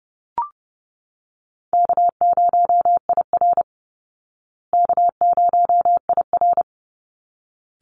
Morse code, heard as a keyed tone, about 30 wpm, 700 Hz. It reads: K0IR K0IR
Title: Easy Call Signs Repeated 2x - 30wpm